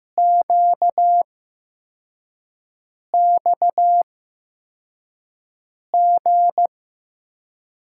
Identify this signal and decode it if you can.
Morse code, heard as a keyed tone, about 15 wpm, spaced as for 4 wpm, 700 Hz. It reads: QXG
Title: USA Call Signs - 5 Characters - 8x Character Spacing - 15wpm